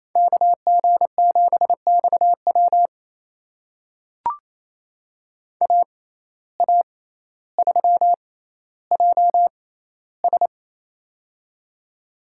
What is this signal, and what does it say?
Morse code, heard as a keyed tone, about 28 wpm, spaced as for 14 wpm, 700 Hz. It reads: KG7XW AA3JS
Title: USA Call Signs - 5 Characters - 6x Character Spacing - 28wpm